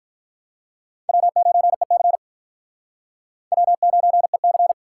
Morse code, heard as a keyed tone, about 46 wpm, 700 Hz. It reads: W9EC W9EC